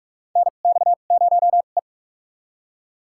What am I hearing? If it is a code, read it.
Morse code, heard as a keyed tone, about 44 wpm, spaced as for 32 wpm, 690 Hz. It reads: NX0E
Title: USA Call Signs - 4 Characters - 2x Character Spacing - 45wpm